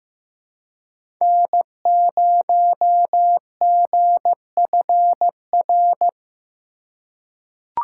Morse code, heard as a keyed tone, about 16 wpm, 700 Hz. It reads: N0GFR
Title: USA Call Signs - 5 Characters - 2x Character Spacing - 15wpm